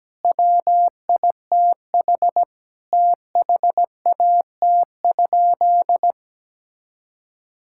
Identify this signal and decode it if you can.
Morse code, heard as a keyed tone, about 17 wpm, 700 Hz. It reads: WITH THAT?
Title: Sentences from Top 1000 Words - Review Part 6 - 17wpm